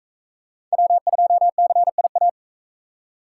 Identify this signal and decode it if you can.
Morse code, heard as a keyed tone, about 42 wpm, 700 Hz. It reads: W2CIA